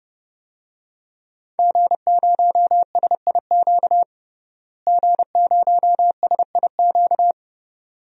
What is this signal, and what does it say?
Morse code, heard as a keyed tone, about 30 wpm, 700 Hz. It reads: G0SIQ G0SIQ
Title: DXCC-100 Call Signs - 5 Characters - Repeated 2x - 30wpm